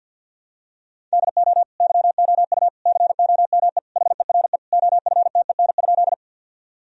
Morse code, heard as a keyed tone, about 52 wpm, 700 Hz. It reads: DO YOU COME HERE OFTEN?
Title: Sentences from Top 1000 Words - Review Part 1 - 50wpm